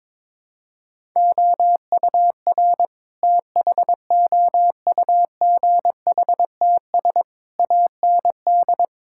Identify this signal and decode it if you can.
Morse code, heard as a keyed tone, about 22 wpm, 700 Hz. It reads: OUR THOUGHTS AND
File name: practice / 22wpm / Sentences with 8-Word Phrases - 22wpm